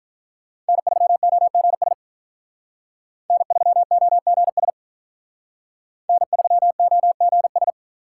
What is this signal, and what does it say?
Morse code, heard as a keyed tone, about 45 wpm, 700 Hz. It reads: N3OGS N3OGS N3OGS